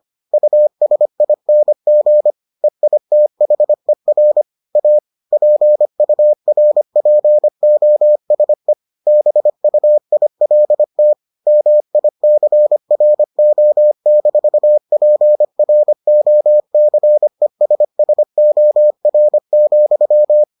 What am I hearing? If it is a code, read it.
Morse code, heard as a keyed tone, about 25 wpm, 600 Hz. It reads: USING EITHER A PURPOSE BUILT MICRO-PROCESSOR,